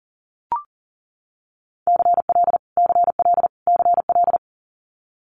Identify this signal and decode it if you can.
Morse code, heard as a keyed tone, about 38 wpm, 700 Hz. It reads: CL CL CL
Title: Multi-letter Phonemes Repeated 3x 40wpm